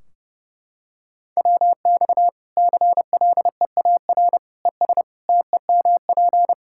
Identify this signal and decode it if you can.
Morse code, heard as a keyed tone, about 30 wpm, 700 Hz. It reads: WX CLEAR ES TEMP
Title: Practice QSO - Exchange 2 - Info - 30wpm